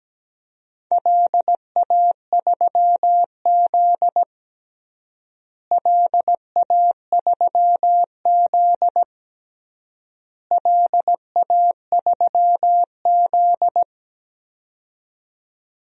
Morse code, heard as a keyed tone, about 17 wpm, 700 Hz. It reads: LA3Z LA3Z LA3Z
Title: DXCC-100 Call Signs - 4 Characters - Repeated 3x - 17wpm